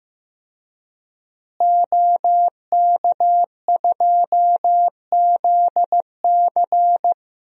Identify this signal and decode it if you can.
Morse code, heard as a keyed tone, about 15 wpm, 700 Hz. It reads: OK2ZC
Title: DXCC-100 Call Signs - 5 Characters - Repeated 2x - 15wpm